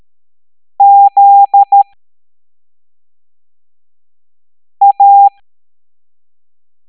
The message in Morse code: ZA